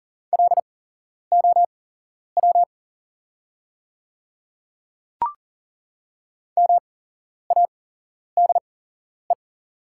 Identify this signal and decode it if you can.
Morse code, heard as a keyed tone, about 42 wpm, spaced as for 11 wpm, 700 Hz. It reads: LOW MADE
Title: 8x Character Spacing - Top 300 Words - 40wpm